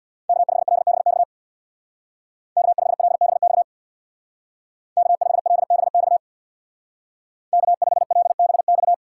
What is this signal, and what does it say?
Morse code, heard as a keyed tone, about 60 wpm, 700 Hz. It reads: K5LBX K5LBX K5LBX K5LBX
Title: Speed Racing - USA 1x3 Call Signs 50wpm